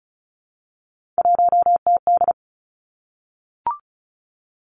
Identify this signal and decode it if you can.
Morse code, heard as a keyed tone, about 36 wpm, 700 Hz. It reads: 1TD
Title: Three Challenging Letter-Number Combinations 35wpm